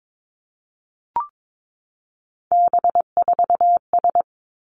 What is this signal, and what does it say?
Morse code, heard as a keyed tone, about 22 wpm, 700 Hz. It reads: B4S
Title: Three Challenging Letter-Number Combinations 22wpm